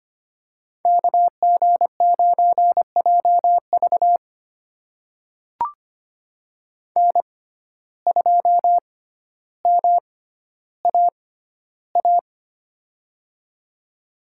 Morse code, heard as a keyed tone, about 25 wpm, spaced as for 13 wpm, 700 Hz. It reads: KG9JV N2MAA
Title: USA Call Signs - 6x Character Spacing - 25wpm